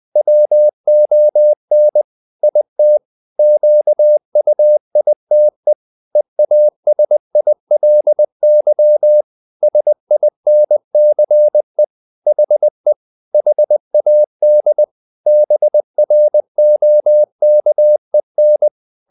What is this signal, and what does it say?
Morse code, heard as a keyed tone, about 20 wpm, 600 Hz. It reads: WON IT QUITE EASILY SINCE HE HAD BROKEN